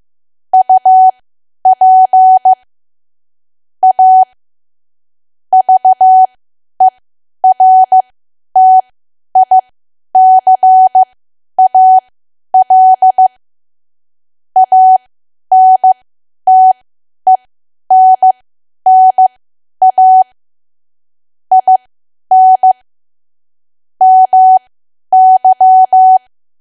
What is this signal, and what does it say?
Morse code, heard as a keyed tone, about 15 wpm, spaced as for 10 wpm, 750 Hz. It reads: UP A VERTICAL ANTENNA IN MY
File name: cwa / W1AW-Level2 / 10WPM